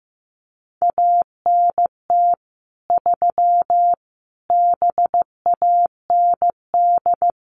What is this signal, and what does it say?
Morse code, heard as a keyed tone, about 15 wpm, 700 Hz. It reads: ANT 3 BAND
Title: Common QSO Phrases 15wpm